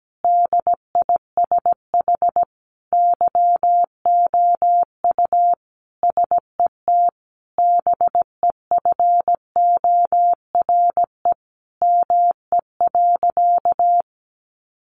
Morse code, heard as a keyed tone, about 17 wpm, 700 Hz. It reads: DISH YOU SET BEFORE ME.
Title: Sentences from Top 1300 Words 17wpm